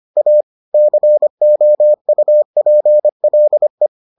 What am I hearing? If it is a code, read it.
Morse code, heard as a keyed tone, about 25 wpm, 600 Hz. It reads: A COUPLE